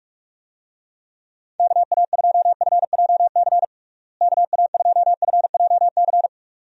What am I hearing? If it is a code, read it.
Morse code, heard as a keyed tone, about 46 wpm, 700 Hz. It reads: KA2FJC KA2FJC
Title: USA Call Signs - 6 Characters - Repeated 2x - 45wpm